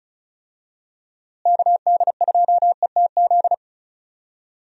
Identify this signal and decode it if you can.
Morse code, heard as a keyed tone, about 36 wpm, 700 Hz. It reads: KD2ETZ
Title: USA Call Signs - 6 Characters - Repeated 3x - 35wpm